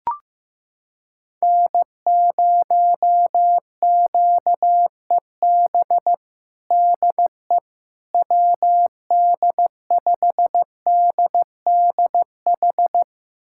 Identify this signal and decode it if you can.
Morse code, heard as a keyed tone, about 15 wpm, 700 Hz. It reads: N0QEB DE WD5DDH
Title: USA Callsign DE Callsign 15wpm